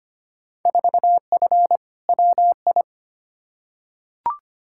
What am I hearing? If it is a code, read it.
Morse code, heard as a keyed tone, about 25 wpm, 700 Hz. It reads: 4F WI
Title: Field Day Exchanges - 25wpm